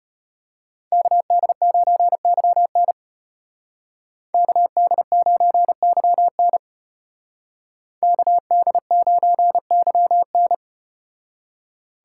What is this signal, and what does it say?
Morse code, heard as a keyed tone, about 34 wpm, 700 Hz. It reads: KD9YN KD9YN KD9YN